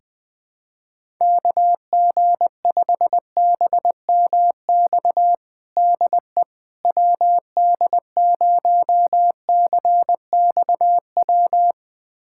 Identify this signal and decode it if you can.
Morse code, heard as a keyed tone, about 20 wpm, 700 Hz. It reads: KG5BMX DE WD0CXW
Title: USA Callsign DE Callsign - 6 Characters - 20wpm